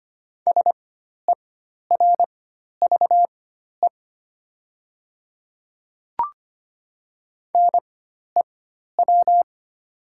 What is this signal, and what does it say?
Morse code, heard as a keyed tone, about 25 wpm, spaced as for 12 wpm, 700 Hz. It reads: SERVE NEW
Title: 4x Character Spacing - Top 500 Words - 25wpm